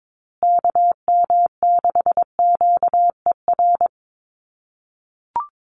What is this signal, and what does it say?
Morse code, heard as a keyed tone, about 22 wpm, 700 Hz. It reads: KM6QER
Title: USA Call Signs Repeated 3x 22wpm